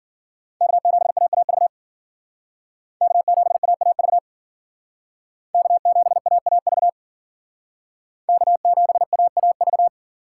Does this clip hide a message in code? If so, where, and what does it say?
Yes: Morse code from 0.6 to 9.9 s, heard as a keyed tone, about 51 wpm, 700 Hz. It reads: K7AAV K7AAV K7AAV K7AAV